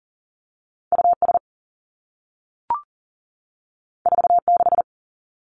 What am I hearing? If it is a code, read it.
Morse code, heard as a keyed tone, about 38 wpm, 700 Hz. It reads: U S 4 6
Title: Two Challenging Letter-Number Combinations 40wpm